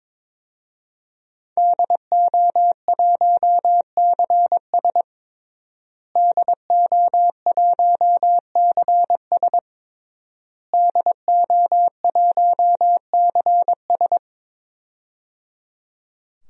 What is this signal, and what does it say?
Morse code, heard as a keyed tone, about 22 wpm, 700 Hz. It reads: DO1CS DO1CS DO1CS
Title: DXCC-100 Call Signs - 5 Characters - Repeated 3x - 22wpm